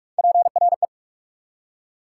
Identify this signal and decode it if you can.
Morse code, heard as a keyed tone, about 46 wpm, 690 Hz. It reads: PRE
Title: Prefixes and Suffixes 45wpm